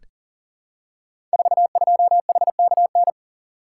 Morse code, heard as a keyed tone, about 43 wpm, 700 Hz. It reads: 42HKN
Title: Code Groups - 5 Letter:Numbers - 40wpm